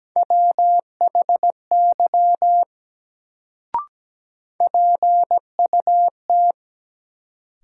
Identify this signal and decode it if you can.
Morse code, heard as a keyed tone, about 17 wpm, 700 Hz. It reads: WHY PUT
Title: Sentences from Top 200 Words - New Word Drill - 17wpm